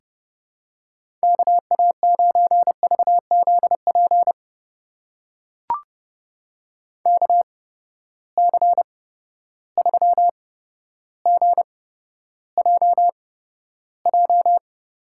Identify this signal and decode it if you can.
Morse code, heard as a keyed tone, about 30 wpm, spaced as for 13 wpm, 700 Hz. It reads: KA9VZP KC3GJJ
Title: USA Call Signs - 6 Characters - 8x Character Spacing - 30wpm